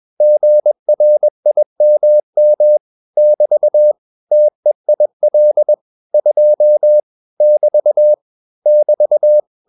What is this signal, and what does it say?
Morse code, heard as a keyed tone, about 21 wpm, 600 Hz. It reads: GRIMM <BT> TEIL 2 <BT> <BT>
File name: marienkind2 21wpm